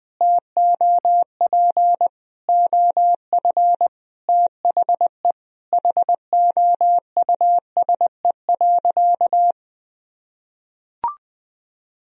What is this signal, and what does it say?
Morse code, heard as a keyed tone, about 20 wpm, 700 Hz. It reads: TOP OF THE HOUSE.
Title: Sentences from Top 1000 Words - Review Part 2 - 20wpm